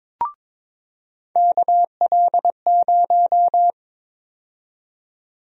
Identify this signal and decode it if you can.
Morse code, heard as a keyed tone, about 22 wpm, 700 Hz. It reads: KL0
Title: USA Call Sign Prefixes - 22wpm